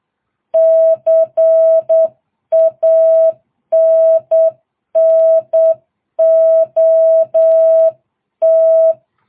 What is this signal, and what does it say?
Morse code, heard as a keyed tone, about 8 wpm, 660 Hz. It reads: CANNOT